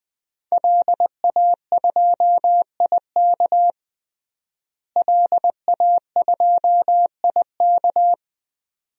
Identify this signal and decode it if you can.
Morse code, heard as a keyed tone, about 21 wpm, 700 Hz. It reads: LA2IK LA2IK
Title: DXCC-100 Call Signs - 5 Characters - Repeated 2x - 20wpm